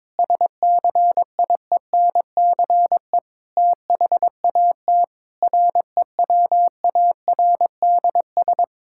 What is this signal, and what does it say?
Morse code, heard as a keyed tone, about 22 wpm, 700 Hz. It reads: SCIENCE THAT REWARDS
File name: Difficult Sentences with Hints - 22wpm